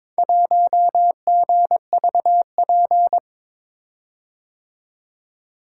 Morse code, heard as a keyed tone, about 22 wpm, 700 Hz. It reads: 1GVP
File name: practice / 22wpm / Four Letter-Number Combinations - 22wpm